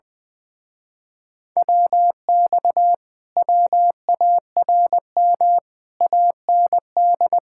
Morse code, heard as a keyed tone, about 20 wpm, 700 Hz. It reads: WX WARM AND
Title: Speed-Racing Common QSO Phrases - 20wpm